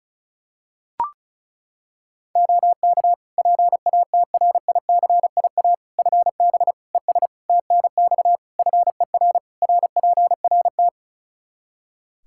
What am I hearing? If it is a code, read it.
Morse code, heard as a keyed tone, about 36 wpm, 700 Hz. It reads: OK PATRICIA FB ES TNX FER RPRT
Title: Practice QSO - Exchange 2 - Info - 35wpm